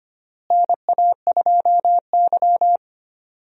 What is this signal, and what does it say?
Morse code, heard as a keyed tone, about 25 wpm, 700 Hz. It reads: NA2Y
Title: USA Call Signs - 4 Characters - Repeated 3x - 25wpm